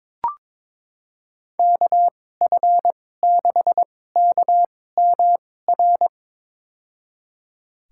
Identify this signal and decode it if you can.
Morse code, heard as a keyed tone, about 22 wpm, spaced as for 16 wpm, 700 Hz. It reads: KF6KMR
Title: USA Call Signs - 2x Character Spacing - 22wpm